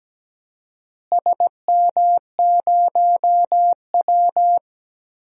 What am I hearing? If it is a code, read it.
Morse code, heard as a keyed tone, about 17 wpm, 700 Hz. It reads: SM0W
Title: DXCC-100 Call Signs - 4 Characters - Repeated 3x - 17wpm